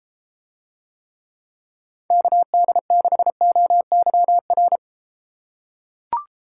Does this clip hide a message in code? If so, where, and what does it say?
Yes: Morse code from 2.1 to 4.8 s, heard as a keyed tone, about 32 wpm, 700 Hz. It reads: KD6OYR